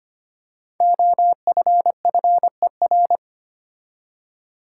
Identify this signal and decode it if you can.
Morse code, heard as a keyed tone, about 25 wpm, 700 Hz. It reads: OFFER
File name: 5 Letter Words - First 3 Letters Given - 25wpm